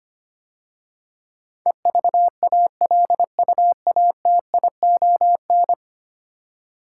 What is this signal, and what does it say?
Morse code, heard as a keyed tone, about 25 wpm, 700 Hz. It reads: EVALUATION
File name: 10 Letter Words - First 3 Letters Given - 25wpm